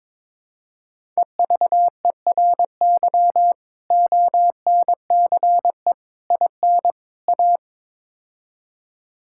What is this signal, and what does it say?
Morse code, heard as a keyed tone, about 21 wpm, 700 Hz. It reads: EVERY ONCE IN A
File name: practice / 22wpm / Build A Phrase - 5-Word Phrases - 22wpm